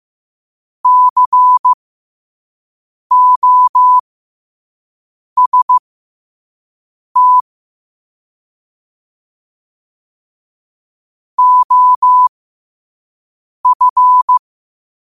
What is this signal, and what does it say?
Morse code, heard as a keyed tone, about 15 wpm, spaced as for 5 wpm, 1000 Hz. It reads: COST OF
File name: qotdfiles / 5wpm